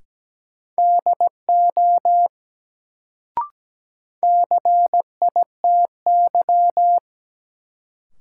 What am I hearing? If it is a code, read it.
Morse code, heard as a keyed tone, about 17 wpm, 700 Hz. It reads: DO CITY
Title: Sets of 2 Words - New Word Drill - 17wpm